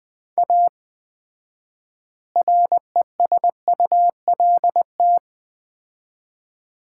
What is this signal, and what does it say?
Morse code, heard as a keyed tone, about 20 wpm, 700 Hz. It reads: A RESULT
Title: 4x Word-Spacing - Sets of 4 Words - 20wpm